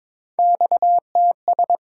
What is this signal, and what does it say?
Morse code, heard as a keyed tone, about 22 wpm, 700 Hz. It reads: XTS
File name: practice / 22wpm / Three Letter-Number Combinations - 22wpm